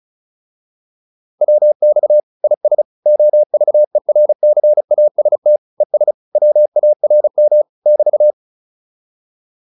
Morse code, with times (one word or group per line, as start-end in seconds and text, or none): 1.4-2.2 WX
2.4-2.8 IS
3.1-5.6 OVERCAST
5.8-6.1 ES
6.4-7.6 WARM
7.9-8.3 <BT>